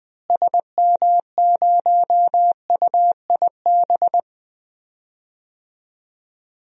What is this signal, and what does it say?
Morse code, heard as a keyed tone, about 20 wpm, 700 Hz. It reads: SM0UIB
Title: DXCC-100 Call Signs - 6 Characters - Repeated 3x - 20wpm